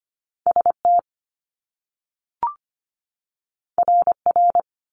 Morse code, heard as a keyed tone, about 25 wpm, 700 Hz. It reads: ST RR
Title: Top Two Letter Combinations in Words 25wpm